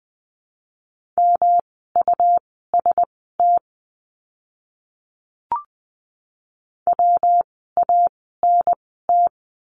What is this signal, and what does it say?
Morse code, heard as a keyed tone, about 20 wpm, spaced as for 15 wpm, 700 Hz. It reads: MUST WANT
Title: 237 Lesson 38f - Letter q - Words 20wpm